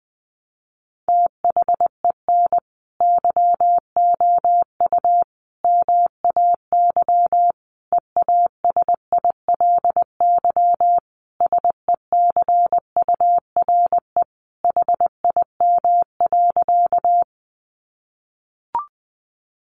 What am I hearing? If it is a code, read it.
Morse code, heard as a keyed tone, about 19 wpm, 700 Hz. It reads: THEN YOU MAY EASILY SECURE HIM.
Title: Sentences from Top 1200 Words 20wpm